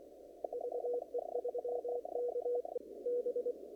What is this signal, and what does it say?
Morse code, heard as a keyed tone, about 24 wpm, spaced as for 20 wpm, 490 Hz. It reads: VE3KB
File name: CW / kt5j wpx cw-op-k5pi